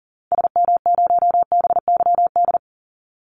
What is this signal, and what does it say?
Morse code, heard as a keyed tone, about 40 wpm, 700 Hz. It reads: SM0BYD
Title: Call Signs - 40wpm